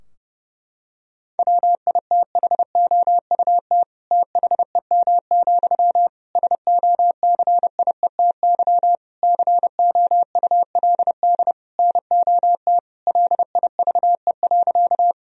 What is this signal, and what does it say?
Morse code, heard as a keyed tone, about 30 wpm, 700 Hz. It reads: WITHOUT THEM, SOCIETY COULD NOT LIVE.
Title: Sentences from Top 1000 Words - Review Part 4 - 30wpm